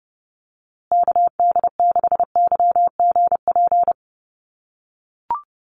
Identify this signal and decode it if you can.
Morse code, heard as a keyed tone, about 30 wpm, 700 Hz. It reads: KD6YGP
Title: USA Call Signs Repeated 3x 30wpm